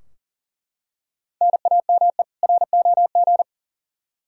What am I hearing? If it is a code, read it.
Morse code, heard as a keyed tone, about 37 wpm, 700 Hz. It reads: NAME ROG
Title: Practice QSO - Exchange 1 - Beginning - 40wpm